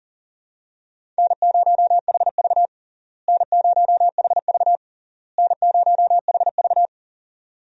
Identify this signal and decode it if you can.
Morse code, heard as a keyed tone, about 39 wpm, 700 Hz. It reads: N0HV N0HV N0HV
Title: USA Call Signs - 4 Characters - Repeated 3x - 40wpm